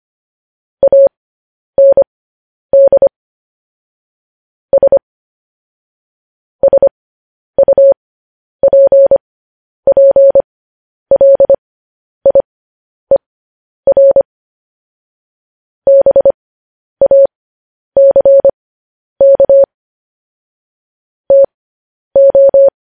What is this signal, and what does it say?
Morse code, heard as a keyed tone, about 25 wpm, spaced as for 10 wpm, 550 Hz. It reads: AND S SUPPLIER BACK TO
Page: news-headlines-in-morse-code-at-10-wpm